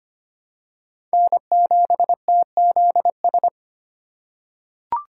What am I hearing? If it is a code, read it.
Morse code, heard as a keyed tone, about 25 wpm, 700 Hz. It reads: N7TZS
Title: USA Call Signs - 5 Characters - 4x Character Spacing - 25wpm